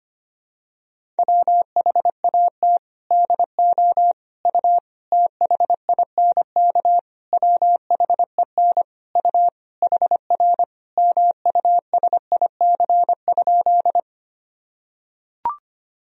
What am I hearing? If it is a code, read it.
Morse code, heard as a keyed tone, about 25 wpm, 700 Hz. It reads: WHAT DO U THINK WHEN U HR MUSIC?
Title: Sentences from Top 300 Words and Abbreviations 25wpm